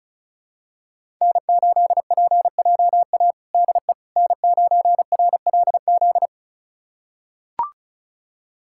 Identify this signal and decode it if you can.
Morse code, heard as a keyed tone, about 34 wpm, 700 Hz. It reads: N8PJA DE N9RLZ
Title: USA Callsign DE Callsign - 5 Characters - 35wpm